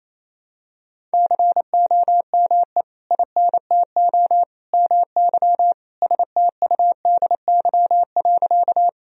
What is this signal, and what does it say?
Morse code, heard as a keyed tone, about 28 wpm, 700 Hz. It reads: COME INTO MY STUDY.
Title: Sentences from Top 1000 Words - Review Part 1 - 28wpm